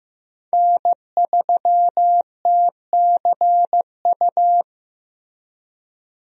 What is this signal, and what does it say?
Morse code, heard as a keyed tone, about 15 wpm, 700 Hz. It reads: N3TCU
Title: USA Call Signs - 5 Characters - Repeated 2x - 15wpm